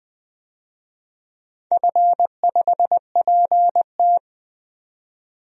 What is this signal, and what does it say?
Morse code, heard as a keyed tone, about 20 wpm, 700 Hz. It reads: F5PT
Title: DXCC-100 Call Signs - 4 Characters - 20wpm